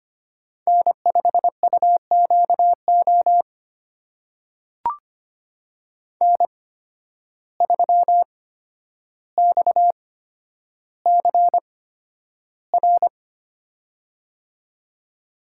Morse code, heard as a keyed tone, about 25 wpm, spaced as for 11 wpm, 700 Hz. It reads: N5UQO N3XCR
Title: USA Call Signs - 5 Characters - 8x Character Spacing - 25wpm